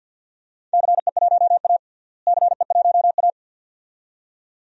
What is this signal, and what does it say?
Morse code, heard as a keyed tone, about 48 wpm, 700 Hz. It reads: CE1A CE1A
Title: DXCC-100 Call Signs - 4 Characters - Repeated 2x - 50wpm